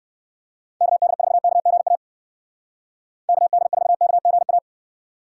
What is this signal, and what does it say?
Morse code, heard as a keyed tone, about 59 wpm, 700 Hz. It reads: KD4KGA KD4KGA